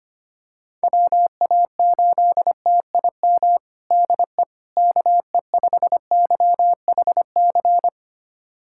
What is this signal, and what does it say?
Morse code, heard as a keyed tone, about 25 wpm, 700 Hz. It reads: WA8TIM DE KE5YHC
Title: USA Callsign DE Callsign 25wpm